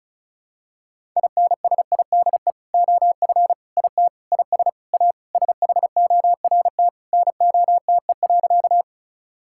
Morse code, heard as a keyed tone, about 36 wpm, 700 Hz. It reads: INSIDE OF IT IS A SHORT NOTE.
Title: Sentences from Top 1000 Words - Review Part 6 - 35wpm